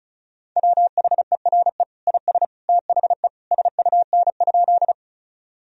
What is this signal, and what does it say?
Morse code, heard as a keyed tone, about 36 wpm, 700 Hz. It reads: WHERE IS THE SUN?
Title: Sentences from Top 1000 Words - Review Part 1 - 35wpm